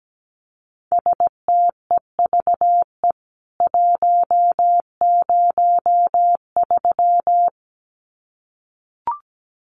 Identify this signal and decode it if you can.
Morse code, heard as a keyed tone, about 17 wpm, 700 Hz. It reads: STEVE 103
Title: CWT Exchange 17wpm